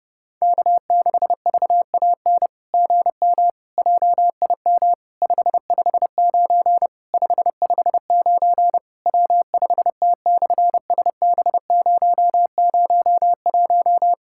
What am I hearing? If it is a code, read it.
Morse code, heard as a keyed tone, about 27 wpm, 700 Hz. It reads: K6VAN GM JIM 559 559 W5T/SB001
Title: SOTA QSO Practice 30wpm